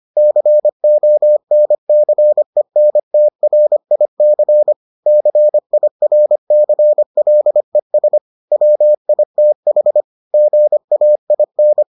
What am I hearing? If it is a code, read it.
Morse code, heard as a keyed tone, about 25 wpm, 600 Hz. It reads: CONCENTRIC CIRCLES WITH GAIN